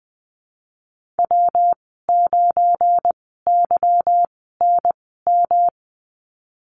Morse code, heard as a keyed tone, about 20 wpm, spaced as for 15 wpm, 700 Hz. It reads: W9YNM
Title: 179 Lesson 29g - Slash - Call Signs 20wpm